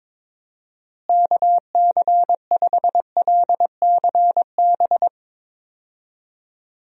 Morse code, heard as a keyed tone, about 22 wpm, 700 Hz. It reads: KC5LCB
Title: USA Call Signs - 6 Characters - Repeated 2x - 22wpm